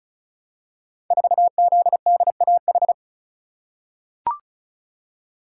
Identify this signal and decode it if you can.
Morse code, heard as a keyed tone, about 36 wpm, 700 Hz. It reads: 4ZDAH
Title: Code Groups - 5 Letter:Numbers - 35wpm